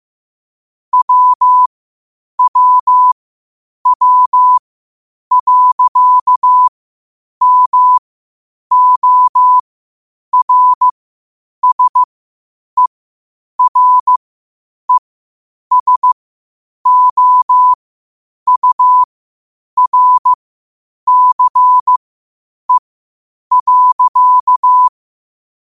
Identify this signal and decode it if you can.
Morse code, heard as a keyed tone, about 15 wpm, spaced as for 8 wpm, 1000 Hz. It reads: WWW.MORSERESOURCE.
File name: qotdfiles / 8wpm